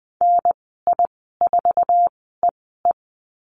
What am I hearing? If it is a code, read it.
Morse code, heard as a keyed tone, about 20 wpm, spaced as for 14 wpm, 700 Hz. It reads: NI4EE
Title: 082 Lesson 14g - Number 2 - Call Signs 20wpm